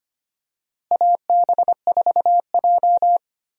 Morse code, heard as a keyed tone, about 25 wpm, 700 Hz. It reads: AB4J
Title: USA Call Signs - 4 Characters - Repeated 3x - 25wpm